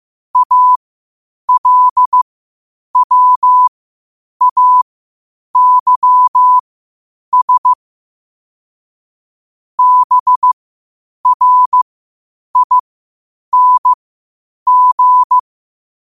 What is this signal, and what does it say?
Morse code, heard as a keyed tone, about 15 wpm, spaced as for 8 wpm, 1000 Hz. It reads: ALWAYS BRING